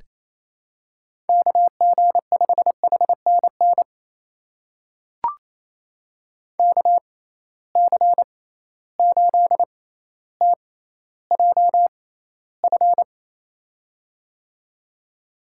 Morse code, heard as a keyed tone, about 28 wpm, spaced as for 14 wpm, 700 Hz. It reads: KG5HNN KC8TJF
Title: USA Call Signs - 6 Characters - 6x Character Spacing - 28wpm